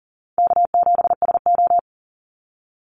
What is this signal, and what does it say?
Morse code, heard as a keyed tone, about 40 wpm, 700 Hz. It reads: K7SO
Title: NA SOTA Activator Callsigns 40wpm